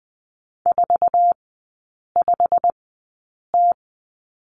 Morse code, heard as a keyed tone, about 20 wpm, spaced as for 8 wpm, 700 Hz. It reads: 45T
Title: 133 Lesson 22e - Question Mark - Three Characters 20wpm